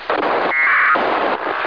Packet radio (AX.25, as in APRS).
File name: ho-68-12-24-2009